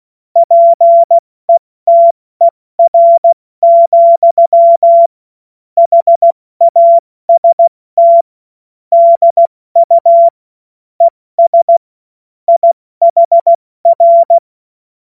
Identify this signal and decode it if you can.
Morse code, heard as a keyed tone, about 16 wpm, 680 Hz. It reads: PETER, HAST DU ES IHR